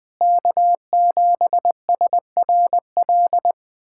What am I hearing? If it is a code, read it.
Morse code, heard as a keyed tone, about 20 wpm, 690 Hz. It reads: K7SRL
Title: SOTA QSO Practice 20wpm